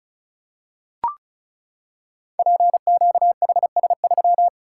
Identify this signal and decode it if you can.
Morse code, heard as a keyed tone, about 36 wpm, 700 Hz. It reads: PQHS3
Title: Code Groups - 5 Letter:Numbers - 35wpm